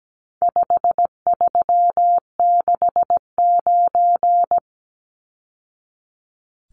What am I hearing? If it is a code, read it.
Morse code, heard as a keyed tone, about 17 wpm, 700 Hz. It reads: 5369
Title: Numbers - 4-Digit 17wpm